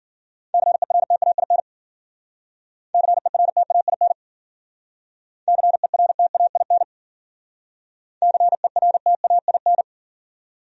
Morse code, heard as a keyed tone, about 51 wpm, 700 Hz. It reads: CERTAIN CERTAIN CERTAIN CERTAIN